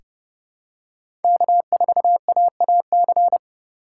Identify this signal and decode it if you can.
Morse code, heard as a keyed tone, about 30 wpm, 700 Hz. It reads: K4AAC
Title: K1USN SST QSO Practice - 30wpm